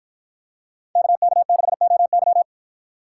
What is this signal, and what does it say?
Morse code, heard as a keyed tone, about 50 wpm, 700 Hz. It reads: KK6OY